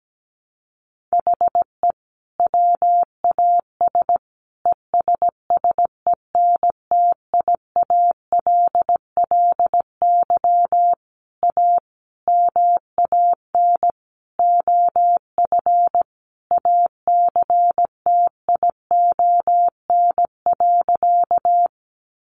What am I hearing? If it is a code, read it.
Morse code, heard as a keyed tone, about 17 wpm, 700 Hz. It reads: HE WAS ESSENTIALLY A MAN OF ACTION.
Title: Sentences from Top 1300 Words 17wpm